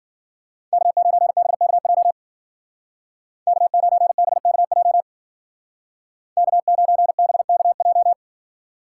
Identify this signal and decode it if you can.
Morse code, heard as a keyed tone, about 52 wpm, 700 Hz. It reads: K9BKJ K9BKJ K9BKJ